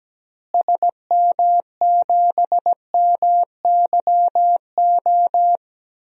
Morse code, heard as a keyed tone, about 17 wpm, 700 Hz. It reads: SM7MYO
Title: DXCC-100 Call Signs - 6 Characters - Repeated 2x - 17wpm